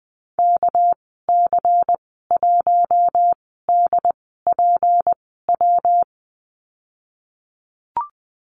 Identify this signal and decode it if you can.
Morse code, heard as a keyed tone, about 20 wpm, spaced as for 15 wpm, 700 Hz. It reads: KC1DPW
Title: 200 Lesson 32g - Letter k - Call Signs 20wpm